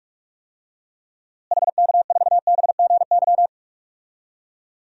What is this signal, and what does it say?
Morse code, heard as a keyed tone, about 42 wpm, 700 Hz. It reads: HK4BGY
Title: DXCC-100 Call Signs - 6 Characters - Repeated 3x - 45wpm